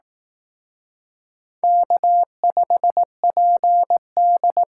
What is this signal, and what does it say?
Morse code, heard as a keyed tone, about 18 wpm, 700 Hz. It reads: K5PD
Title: Speed Racing - USA Call Signs 15wpm